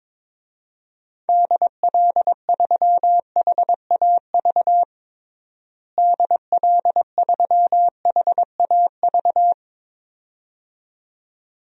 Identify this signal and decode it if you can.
Morse code, heard as a keyed tone, about 22 wpm, 700 Hz. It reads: DL3HAV DL3HAV
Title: DXCC-100 Call Signs - 6 Characters - Repeated 2x - 22wpm